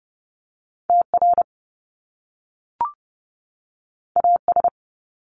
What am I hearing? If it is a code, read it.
Morse code, heard as a keyed tone, about 30 wpm, 700 Hz. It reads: TR AS
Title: Top Two Letter Combinations in Words 30wpm